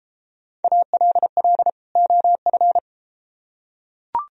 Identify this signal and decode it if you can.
Morse code, heard as a keyed tone, about 32 wpm, 700 Hz. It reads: ALL OF